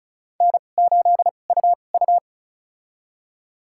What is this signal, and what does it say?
Morse code, heard as a keyed tone, about 34 wpm, spaced as for 25 wpm, 690 Hz. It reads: N8UU
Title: USA Call Signs - 4 Characters - 2x Character Spacing - 35wpm